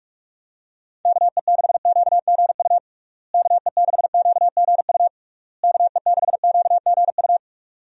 Morse code, heard as a keyed tone, about 47 wpm, 700 Hz. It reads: KE6QGU KE6QGU KE6QGU
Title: USA Call Signs - 6 Characters - Repeated 3x - 45wpm